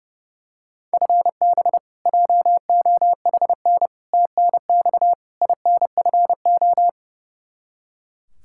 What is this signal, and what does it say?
Morse code, heard as a keyed tone, about 30 wpm, 700 Hz. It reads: FB JOHN TNX INFO
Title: Practice QSO - Exchange 3 - More Info - 30wpm